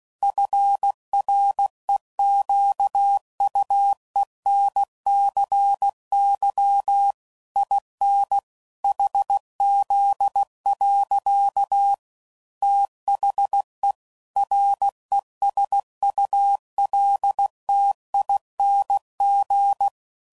Morse code, heard as a keyed tone, about 16 wpm, 800 Hz. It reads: FREQUENCY IN HZ. THE RESULTING
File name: CW Code Groups @15WPM / Technical Subjects / XL